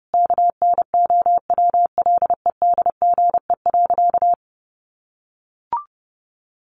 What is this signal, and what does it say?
Morse code, heard as a keyed tone, about 28 wpm, 700 Hz. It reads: KNOWLEDGE.
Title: Sentences from Top 1100 Words 30wpm